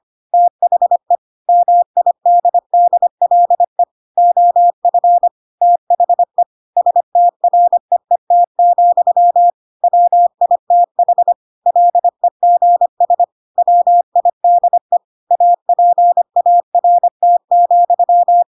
Morse code, heard as a keyed tone, about 25 wpm, 700 Hz. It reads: THE MIDDLE OF THE STREET, WITH LEGS WIDE APART,
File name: Pinocchio / Pinocchio ch3 25 25 ik1hge